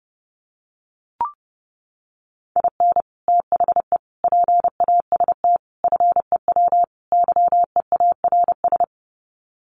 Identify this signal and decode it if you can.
Morse code, heard as a keyed tone, about 30 wpm, 700 Hz. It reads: IN THE PAST FEW YEARS
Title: Sets of 5 Words 30wpm